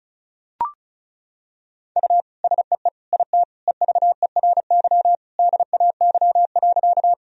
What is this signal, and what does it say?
Morse code, heard as a keyed tone, about 35 wpm, 700 Hz. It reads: U SEE IT EVERY DAY.
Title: Sentences from Top 200 Words and Abbreviations 35wpm